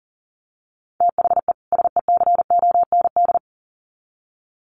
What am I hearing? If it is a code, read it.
Morse code, heard as a keyed tone, about 40 wpm, 700 Hz. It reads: THE SECOND
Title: Sets of 2 Words 40wpm